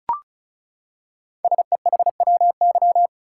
Morse code, heard as a keyed tone, about 36 wpm, 700 Hz. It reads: SEHWY
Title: Code Groups - 5 Letters - 35wpm